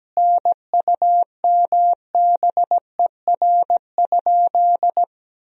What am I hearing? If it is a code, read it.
Morse code, heard as a keyed tone, about 17 wpm, 700 Hz. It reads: NUMBER?
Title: 3x Word-Spacing - Sentences from Top 100 Words - 17wpm